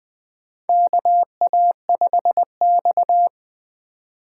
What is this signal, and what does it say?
Morse code, heard as a keyed tone, about 20 wpm, 700 Hz. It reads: KA5X
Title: USA Call Signs - 4 Characters - 4x Character Spacing - 20wpm